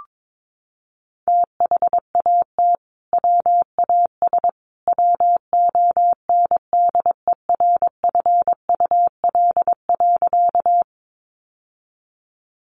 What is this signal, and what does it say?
Morse code, heard as a keyed tone, about 22 wpm, 700 Hz. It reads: THAT WAS WONDERFUL.
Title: Sentences from Top 1100 Words 22wpm